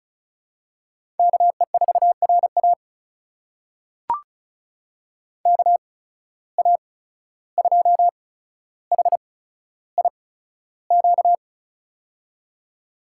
Morse code, heard as a keyed tone, about 36 wpm, spaced as for 14 wpm, 700 Hz. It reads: KE4RA KA2HIQ
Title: USA Call Signs - 8x Character Spacing - 35wpm